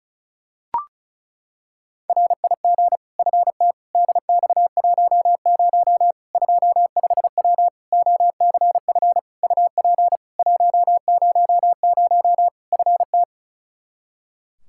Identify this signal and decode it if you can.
Morse code, heard as a keyed tone, about 36 wpm, 700 Hz. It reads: RIG FT DX10 25W OCF UP 100 FT
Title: Practice QSO - Exchange 2 - Info - 35wpm